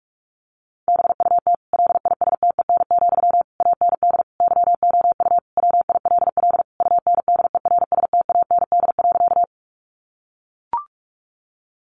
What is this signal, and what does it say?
Morse code, heard as a keyed tone, about 44 wpm, 700 Hz. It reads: BUT LISTEN, AND YOU WILL UNDERSTAND.
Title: Sentences from Top 600 Words 45wpm